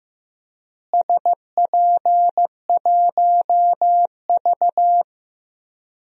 Morse code, heard as a keyed tone, about 15 wpm, 700 Hz. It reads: SP1V
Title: DXCC-100 Call Signs - 4 Characters - Repeated 3x - 15wpm